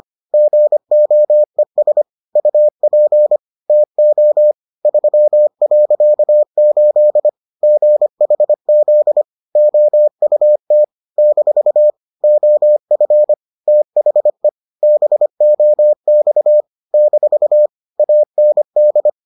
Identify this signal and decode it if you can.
Morse code, heard as a keyed tone, about 25 wpm, 600 Hz. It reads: GOES UP TO 3.8 GHZ OUT - OF THE BOX - AND